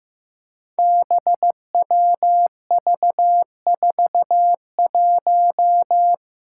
Morse code, heard as a keyed tone, about 15 wpm, 700 Hz. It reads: BWV41
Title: Five Challenging Letter-Number Combinations 15wpm